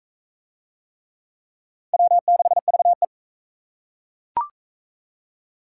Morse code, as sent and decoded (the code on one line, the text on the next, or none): .-- -.... ...- .
W6VE